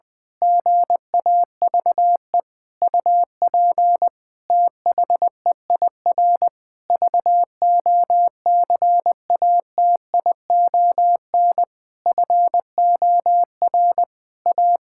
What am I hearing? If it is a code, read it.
Morse code, heard as a keyed tone, about 20 wpm, 700 Hz. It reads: GAVE UP THEIR VOCATION FOR A
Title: Pangram Challenge v2 - 20wpm